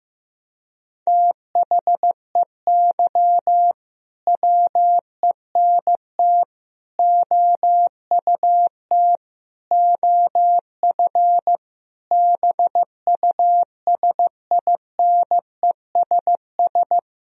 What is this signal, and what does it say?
Morse code, heard as a keyed tone, about 15 wpm, 700 Hz. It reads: THEY WENT OUT OF BUSINESS
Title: Sentences from Top 1000 Words - Review Part 4 - 15wpm